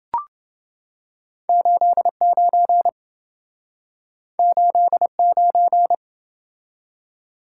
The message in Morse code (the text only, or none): 89 89